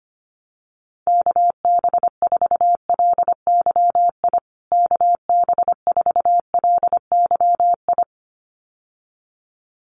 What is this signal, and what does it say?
Morse code, heard as a keyed tone, about 25 wpm, 700 Hz. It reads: KB4LYI KB4LYI
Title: USA Call Signs Repeated 2x 25wpm